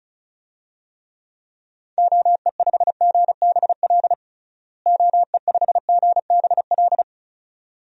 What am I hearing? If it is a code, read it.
Morse code, heard as a keyed tone, about 36 wpm, 700 Hz. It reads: OE5GBL OE5GBL
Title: DXCC-100 Call Signs - 6 Characters - Repeated 2x - 35wpm